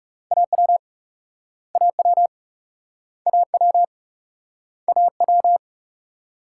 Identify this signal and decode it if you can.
Morse code, heard as a keyed tone, about 41 wpm, 700 Hz. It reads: AW AW AW AW